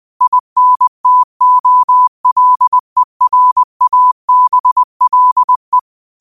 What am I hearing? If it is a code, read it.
Morse code, heard as a keyed tone, about 20 wpm, 1000 Hz. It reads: INTOLERABLE
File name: qotdfiles / 20wpm